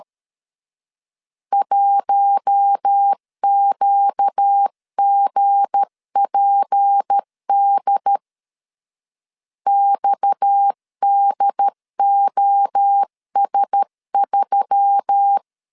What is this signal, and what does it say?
Morse code, heard as a keyed tone, about 13 wpm, 800 Hz. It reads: 1QGPD XDOS3
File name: LEZIONI / 7 GF9Z / 7 G 12wpm 2055